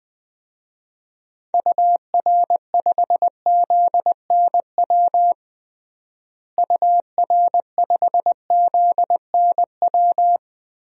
Morse code, heard as a keyed tone, about 21 wpm, 700 Hz. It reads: UR5ZNW UR5ZNW
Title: DXCC-100 Call Signs - 6 Characters - Repeated 3x - 20wpm